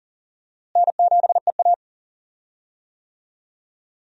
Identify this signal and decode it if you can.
Morse code, heard as a keyed tone, about 40 wpm, 700 Hz. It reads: N7EA
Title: USA Call Signs - 4 Characters - Repeated 2x - 40wpm